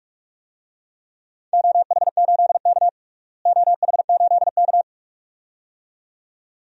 Morse code, heard as a keyed tone, about 46 wpm, 700 Hz. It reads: OH8K OH8K
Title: DXCC-100 Call Signs - 4 Characters - Repeated 2x - 45wpm